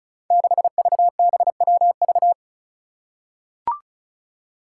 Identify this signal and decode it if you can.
Morse code, heard as a keyed tone, about 36 wpm, 700 Hz. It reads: 6VBWV
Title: Five Challenging Letter-Number Combinations 35wpm